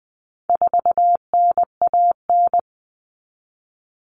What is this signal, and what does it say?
Morse code, heard as a keyed tone, about 20 wpm, 700 Hz. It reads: 4NAN
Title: Common Cut Numbers - 4-Digit 20wpm